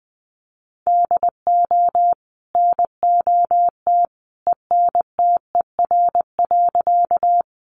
Morse code, heard as a keyed tone, about 20 wpm, 700 Hz. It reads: DO NOT ENTER.
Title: Sentences from Top 1300 Words 20wpm